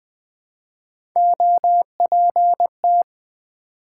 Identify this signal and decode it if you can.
Morse code, heard as a keyed tone, about 20 wpm, 700 Hz. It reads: OPT
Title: Build A Word - 1 Letter At A Time - 3-Letter Words - 20wpm